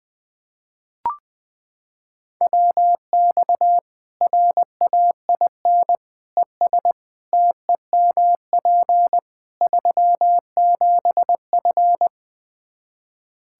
Morse code, heard as a keyed tone, about 21 wpm, 700 Hz. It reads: WX RAIN ES TEMP 37F
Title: Practice QSO - Exchange 2 - Info - 20wpm